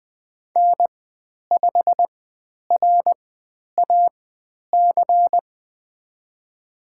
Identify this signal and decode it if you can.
Morse code, heard as a keyed tone, about 20 wpm, spaced as for 10 wpm, 700 Hz. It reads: N5RAC
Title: 095 Lesson 16g - Letter c - Call Signs 20wpm